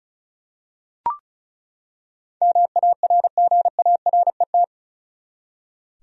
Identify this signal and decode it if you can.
Morse code, heard as a keyed tone, about 36 wpm, 700 Hz. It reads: MARGARET
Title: USA Common Names - 35wpm